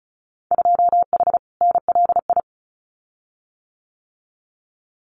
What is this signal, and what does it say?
Morse code, heard as a keyed tone, about 34 wpm, 700 Hz. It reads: 2H NLI
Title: Winter Field Day Exchanges Encore 35wpm